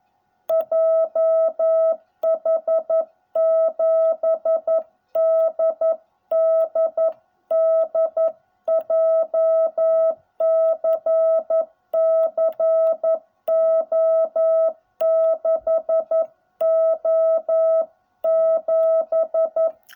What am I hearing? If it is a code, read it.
Morse code, heard as a keyed tone, about 11 wpm, 660 Hz. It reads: JH7DDD JCC O6O7